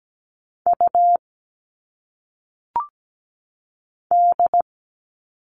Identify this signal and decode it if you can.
Morse code, heard as a keyed tone, about 17 wpm, 700 Hz. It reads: U D
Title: All Cut Numbers - 1-Digit 17wpm